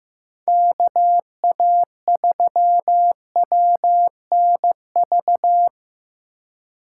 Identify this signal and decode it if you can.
Morse code, heard as a keyed tone, about 17 wpm, 700 Hz. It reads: KA3WNV
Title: Speed Racing - USA 2x3 Call Signs 15wpm